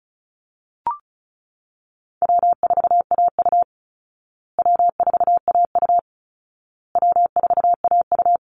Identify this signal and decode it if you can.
Morse code, heard as a keyed tone, about 34 wpm, 700 Hz. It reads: W4AU W4AU W4AU
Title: Easy Call Signs Repeated 3x - 35wpm